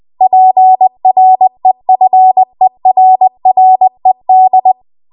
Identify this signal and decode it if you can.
Morse code, heard as a keyed tone, about 20 wpm, 750 Hz. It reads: PREFERRED
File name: Archive / 20 WPM / 20WPM